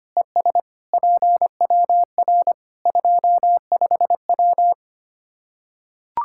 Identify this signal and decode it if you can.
Morse code, heard as a keyed tone, about 25 wpm, 700 Hz. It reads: ES PWR 25W
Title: K1USN SST QSO Practice 25wpm